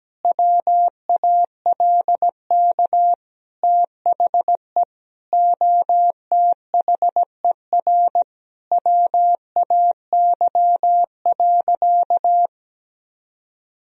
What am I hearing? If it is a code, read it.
Morse code, heard as a keyed tone, about 17 wpm, 700 Hz. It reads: WALK THE OTHER WAY.
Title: Sentences from Top 1000 Words - Review Part 1 - 17wpm